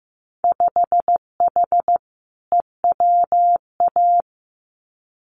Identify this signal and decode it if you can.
Morse code, heard as a keyed tone, about 15 wpm, 700 Hz. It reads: 5H EWA
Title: Winter Field Day Exchanges Encore 15wpm